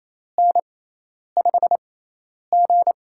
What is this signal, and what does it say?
Morse code, heard as a keyed tone, about 28 wpm, spaced as for 10 wpm, 700 Hz. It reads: N5G
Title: USA Call Signs - 4 Characters - 6x Character Spacing - 28wpm